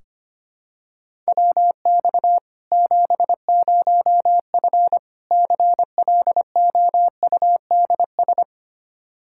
Morse code, heard as a keyed tone, about 25 wpm, 700 Hz. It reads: WX 70F CLOUDS
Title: Practice QSO - Exchange 2 - Info - 25wpm